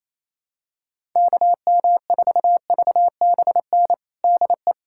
Morse code, heard as a keyed tone, about 27 wpm, 700 Hz. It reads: KM4VBN DE
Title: USA Callsign DE Callsign - 6 Characters - 28wpm